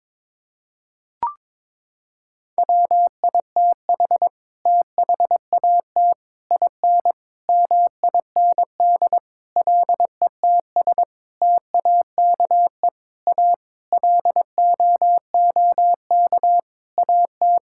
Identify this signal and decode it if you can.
Morse code, heard as a keyed tone, about 22 wpm, 700 Hz. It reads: WITH THAT IN MIND LETS TAKE A LOOK AT
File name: Sets of 9 Words - 22wpm